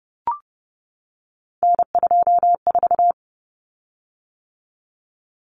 Morse code, heard as a keyed tone, about 30 wpm, 700 Hz. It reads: N24
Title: Common Cut Numbers - 3-Digit 30wpm